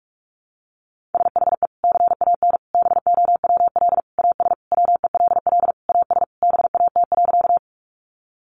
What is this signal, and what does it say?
Morse code, heard as a keyed tone, about 44 wpm, 700 Hz. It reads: SHE CAN BOWL AS WELL AS BAT.
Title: Sentences from Top 1500 Words 45wpm